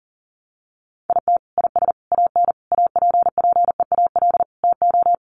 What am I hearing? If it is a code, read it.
Morse code, heard as a keyed tone, about 37 wpm, 700 Hz. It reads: IT IS AN APPEAL TO
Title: Sentences from Top 1300 Words 40wpm